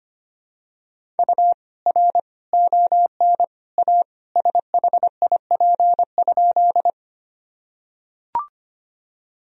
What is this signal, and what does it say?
Morse code, heard as a keyed tone, about 25 wpm, 700 Hz. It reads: U R ON A SHIP?
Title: Sentences from Top 400 Words and Abbreviations 25wpm